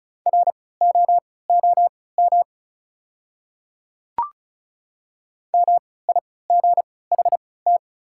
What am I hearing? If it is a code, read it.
Morse code, heard as a keyed tone, about 34 wpm, spaced as for 20 wpm, 700 Hz. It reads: ROOM MIGHT
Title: 3x Character Spacing - Top 300 Words - 35wpm